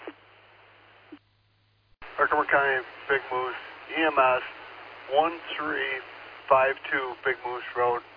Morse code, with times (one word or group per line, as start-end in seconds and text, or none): none